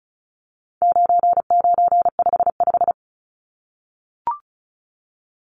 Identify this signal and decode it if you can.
Morse code, heard as a keyed tone, about 34 wpm, 700 Hz. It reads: 9955
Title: Numbers - 4-Digit 35wpm